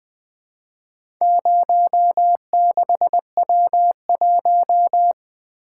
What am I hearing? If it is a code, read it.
Morse code, heard as a keyed tone, about 20 wpm, 700 Hz. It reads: 06W1
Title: Four Letter-Number Combinations - 20wpm